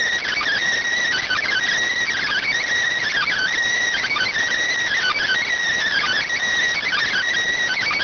Emergency Alert System (SAME) tones (header bursts), heard as harsh buzzing bursts. It is heard from 0.4 to 7.1 s.